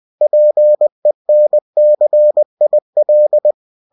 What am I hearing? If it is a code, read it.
Morse code, heard as a keyed tone, about 20 wpm, 600 Hz. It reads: PENCIL
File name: cw-books / winnie-the-pooh / winnie-20wpm-page-0092